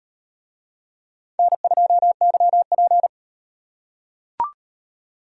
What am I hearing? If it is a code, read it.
Morse code, heard as a keyed tone, about 37 wpm, 700 Hz. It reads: N2YP